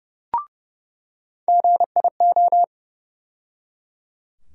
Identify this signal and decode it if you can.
Morse code, heard as a keyed tone, about 30 wpm, 700 Hz. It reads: GIO
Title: USA Call Sign Postfixes - 3 Letters 30wpm